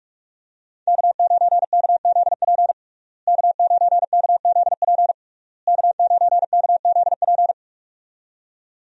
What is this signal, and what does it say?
Morse code, heard as a keyed tone, about 42 wpm, 700 Hz. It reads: K9KZP K9KZP K9KZP
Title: USA Call Signs - 5 Characters - Repeated 3x - 45wpm